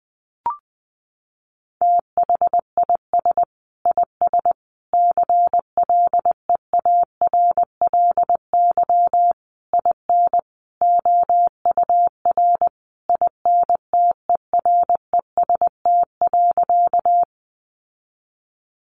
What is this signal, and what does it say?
Morse code, heard as a keyed tone, about 20 wpm, 700 Hz. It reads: THIS IS CLEARLY IN OUR INTEREST.
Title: Sentences from Top 1300 Words 20wpm